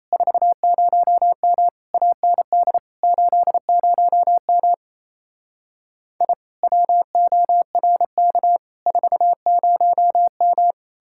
Morse code, heard as a keyed tone, about 38 wpm, 700 Hz. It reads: 40M AND 80M I WORK 40M